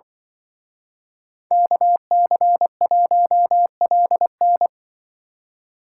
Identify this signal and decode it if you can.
Morse code, heard as a keyed tone, about 25 wpm, 700 Hz. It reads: KC1LN
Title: Speed Racing - USA 2x2 Call Signs 20wpm